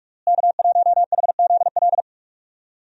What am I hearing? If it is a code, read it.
Morse code, heard as a keyed tone, about 46 wpm, 700 Hz. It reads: K1HZL
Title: USA Call Signs - 5 Characters - Repeated 3x - 45wpm